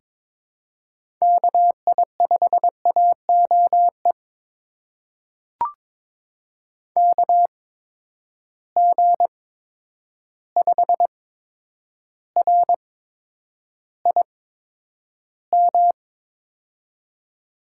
Morse code, heard as a keyed tone, about 22 wpm, spaced as for 9 wpm, 700 Hz. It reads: KI5AOE KG5RIM
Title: USA Call Signs - 6 Characters - 8x Character Spacing - 22wpm